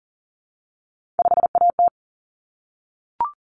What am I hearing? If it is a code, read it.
Morse code, heard as a keyed tone, about 40 wpm, 700 Hz. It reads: 5AT
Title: Common Cut Numbers - 3-Digit 40wpm